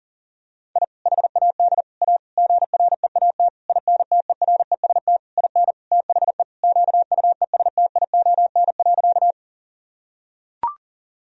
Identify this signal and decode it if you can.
Morse code, heard as a keyed tone, about 40 wpm, 700 Hz. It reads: I HAD A GREAT INTEREST IN THE QUESTION.
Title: Sentences from Top 1000 Words - Review Part 2 - 40wpm